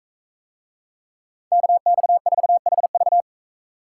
Morse code, heard as a keyed tone, about 42 wpm, 700 Hz. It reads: KX4HV